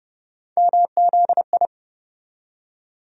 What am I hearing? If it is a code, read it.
Morse code, heard as a keyed tone, about 30 wpm, 700 Hz. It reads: MZI
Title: USA Call Sign Postfixes - 3 Letters 30wpm